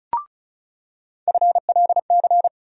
Morse code, heard as a keyed tone, about 35 wpm, 700 Hz. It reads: FLC
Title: USA Call Sign Postfixes - 3 Letters 35wpm